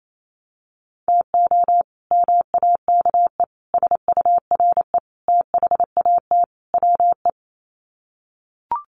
Morse code, heard as a keyed tone, about 28 wpm, 700 Hz. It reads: TO MAKE SURE THAT WE
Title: Sets of 5 Words Encore 28wpm